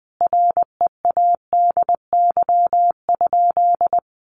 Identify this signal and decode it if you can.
Morse code, heard as a keyed tone, about 20 wpm, 700 Hz. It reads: READY?
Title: Sentences from Top 400 Words 20wpm